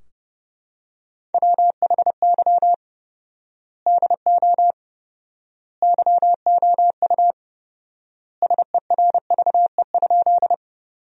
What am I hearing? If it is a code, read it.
Morse code, heard as a keyed tone, about 30 wpm, 700 Hz. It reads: WHY DO YOU SERVE?
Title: 4x Word-Spacing - Sentences from Top 400 Words - 30wpm